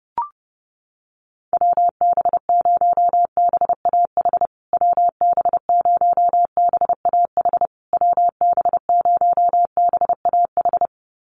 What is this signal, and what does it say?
Morse code, heard as a keyed tone, about 30 wpm, 700 Hz. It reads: WB0BAH WB0BAH WB0BAH
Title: USA Call Signs Repeated 3x 30wpm